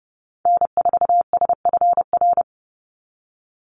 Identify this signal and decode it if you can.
Morse code, heard as a keyed tone, about 30 wpm, 700 Hz. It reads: N4SFR
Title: NA SOTA Activator Callsigns 30wpm